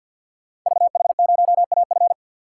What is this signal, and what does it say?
Morse code, heard as a keyed tone, about 47 wpm, 700 Hz. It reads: VH0AF
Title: Code Groups - 5 Letter:Numbers - 50wpm